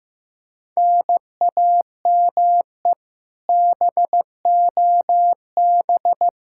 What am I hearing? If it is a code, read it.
Morse code, heard as a keyed tone, about 15 wpm, 700 Hz. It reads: NAME BOB
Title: Practice QSO - Exchange 1 - Beginning - 15wpm